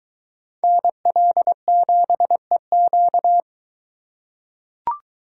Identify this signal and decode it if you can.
Morse code, heard as a keyed tone, about 21 wpm, 700 Hz. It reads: NL7EQ